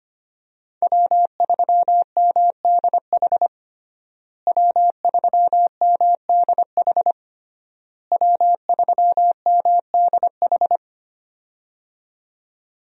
Morse code, heard as a keyed tone, about 25 wpm, 700 Hz. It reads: W3MDH W3MDH W3MDH
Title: USA Call Signs - 5 Characters - Repeated 3x - 25wpm